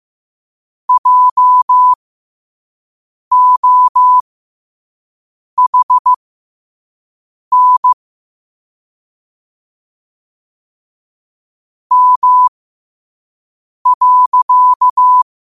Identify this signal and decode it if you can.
Morse code, heard as a keyed tone, about 15 wpm, spaced as for 5 wpm, 1000 Hz. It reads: JOHN M.
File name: qotdfiles / 5wpm